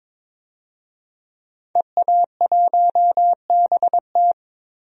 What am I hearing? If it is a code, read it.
Morse code, heard as a keyed tone, about 22 wpm, 700 Hz. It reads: EA1BT
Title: DXCC-100 Call Signs - 5 Characters - Repeated 2x - 22wpm